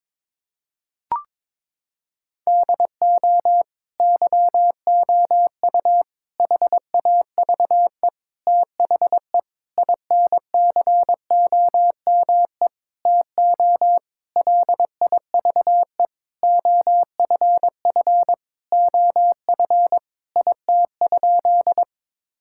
Morse code, heard as a keyed tone, about 22 wpm, 700 Hz. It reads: DO YOU HAVE THE INCOME TO LIVE OFF OF IT?
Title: Sentences from Top 1000 Words - Review Part 5 - 22wpm